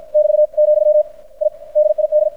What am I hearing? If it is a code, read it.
Morse code, heard as a keyed tone, about 53 wpm, 600 Hz. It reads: <BT> , I DID
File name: archived / CRSnet